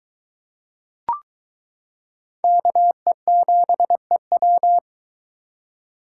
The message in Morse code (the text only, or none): KE7EW